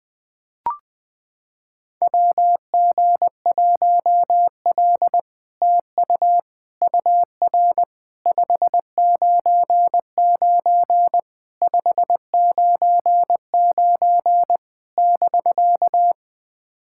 Morse code, heard as a keyed tone, about 20 wpm, 700 Hz. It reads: WG1L TU UR 599 599 <BK>
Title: SOTA QSO Practice 20wpm